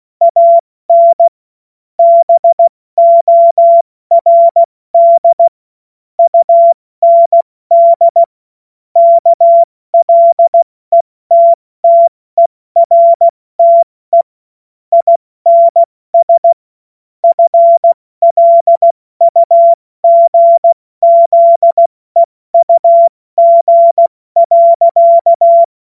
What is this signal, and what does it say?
Morse code, heard as a keyed tone, about 16 wpm, spaced as for 14 wpm, 680 Hz. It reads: AN BORD UND KLETTERTE INS FLUGZEUG.
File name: Morsekurs / Witze / Witz 16 Januar 2024 70 F80